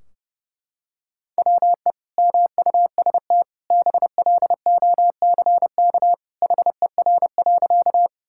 Morse code, heard as a keyed tone, about 30 wpm, 700 Hz. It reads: WE MUST BLOCK HER.
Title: Sentences from Top 1600 Words 30wpm